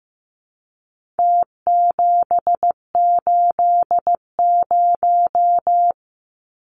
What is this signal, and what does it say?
Morse code, heard as a keyed tone, about 15 wpm, 700 Hz. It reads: T780
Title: Numbers - Contest Style - 4-Digit 15wpm